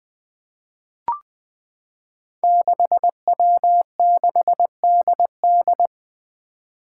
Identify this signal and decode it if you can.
Morse code, heard as a keyed tone, about 20 wpm, 700 Hz. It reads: 6W6DD
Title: Five Challenging Letter-Number Combinations 20wpm